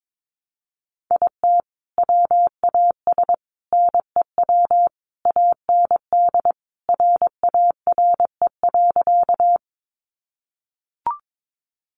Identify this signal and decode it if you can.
Morse code, heard as a keyed tone, about 22 wpm, 700 Hz. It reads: IT WAS NEW AND RARE.
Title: Sentences from Top 1100 Words 22wpm